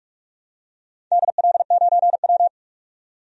Morse code, heard as a keyed tone, about 46 wpm, 700 Hz. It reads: DL9W
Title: DXCC-100 Call Signs - 4 Characters - 45wpm